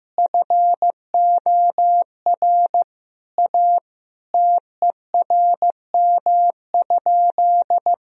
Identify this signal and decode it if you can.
Morse code, heard as a keyed tone, about 15 wpm, 700 Hz. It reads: FOR A TERM?
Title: Sentences from Top 1000 Words - Review Part 4 - 15wpm